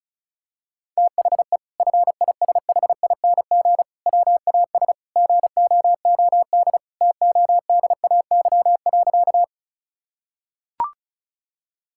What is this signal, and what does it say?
Morse code, heard as a keyed tone, about 36 wpm, 700 Hz. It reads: THE FISHING WAS GOOD TODAY.
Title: Sentences from Top 1000 Words - Review Part 5 - 35wpm